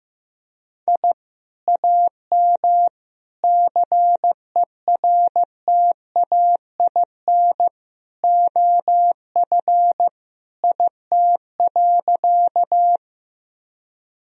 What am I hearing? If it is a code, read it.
Morse code, heard as a keyed tone, about 15 wpm, 700 Hz. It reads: I AM CERTAIN OF IT.
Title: Sentences from Top 1000 Words - Review Part 2 - 15wpm